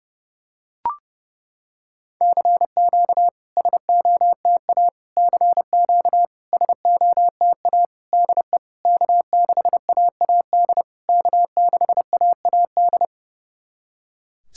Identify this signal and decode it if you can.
Morse code, heard as a keyed tone, about 30 wpm, 690 Hz. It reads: CQ SOTA CQ SOTA DE K6AAD K6AAD
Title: SOTA QSO Practice 30wpm